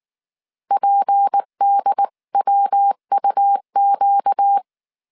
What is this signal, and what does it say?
Morse code, heard as a keyed tone, about 19 wpm, 800 Hz. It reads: PDWUQ
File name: CW / DET / 1 6 1500C S C 18wpm 2356